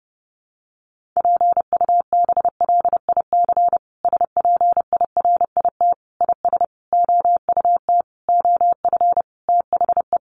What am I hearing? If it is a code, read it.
Morse code, heard as a keyed tone, about 30 wpm, 700 Hz. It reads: PUBLIC SPIRIT IS OUT OF THE
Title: Sentences from Top 1400 Words 30wpm